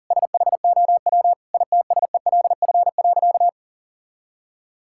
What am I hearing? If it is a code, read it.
Morse code, heard as a keyed tone, about 40 wpm, 700 Hz. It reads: SHOW ITSELF.
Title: Sentences from Top 1000 Words - Review Part 5 - 40wpm